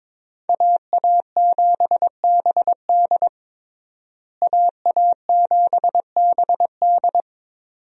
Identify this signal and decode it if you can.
Morse code, heard as a keyed tone, about 22 wpm, 700 Hz. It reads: AA7BD AA7BD
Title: USA Call Signs - 5 Characters - Repeated 2x - 22wpm